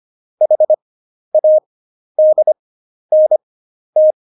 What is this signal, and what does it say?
Morse code, heard as a keyed tone, about 25 wpm, spaced as for 11 wpm, 620 Hz. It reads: HADNT
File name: Jokes-25-char-10-eff